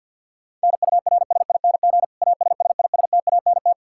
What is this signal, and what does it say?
Morse code, heard as a keyed tone, about 53 wpm, 700 Hz. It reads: NURSING ASSISTANT
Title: Practice QSO - Exchange 3 - More Info - 50wpm